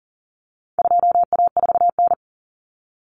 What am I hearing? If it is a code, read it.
Morse code, heard as a keyed tone, about 40 wpm, 700 Hz. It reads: 2A4N
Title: Common Cut Numbers - 4-Digit 40wpm